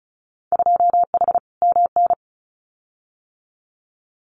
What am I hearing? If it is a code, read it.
Morse code, heard as a keyed tone, about 34 wpm, 700 Hz. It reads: 2H MN
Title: Winter Field Day Exchanges 35wpm